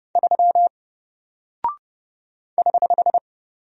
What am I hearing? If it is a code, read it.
Morse code, heard as a keyed tone, about 30 wpm, 700 Hz. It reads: 3 <HH>
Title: Single Character (Letter:Number:Punctuation:Prosign) 30wpm